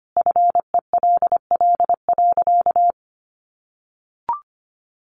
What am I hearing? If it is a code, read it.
Morse code, heard as a keyed tone, about 25 wpm, 700 Hz. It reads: FELL.
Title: A Sound Of Thunder 25wpm